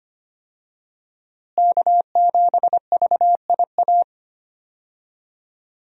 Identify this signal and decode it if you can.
Morse code, heard as a keyed tone, about 25 wpm, 700 Hz. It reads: K7VIA
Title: USA Call Signs - 5 Characters - Repeated 2x - 25wpm